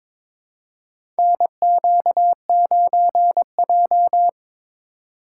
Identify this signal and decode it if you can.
Morse code, heard as a keyed tone, about 23 wpm, 700 Hz. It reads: NQ9J
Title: USA Call Signs - 2x Character Spacing - 22wpm